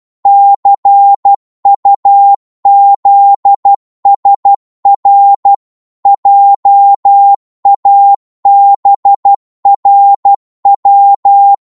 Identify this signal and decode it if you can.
Morse code, heard as a keyed tone, about 12 wpm, 800 Hz. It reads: CUZSR JABRW
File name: Morse / ltr-12wpm